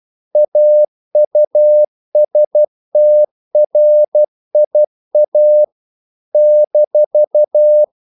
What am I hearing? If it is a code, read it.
Morse code, heard as a keyed tone, about 12 wpm, 600 Hz. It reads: AUSTRIA -
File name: CW / Practice / NationalCapitals12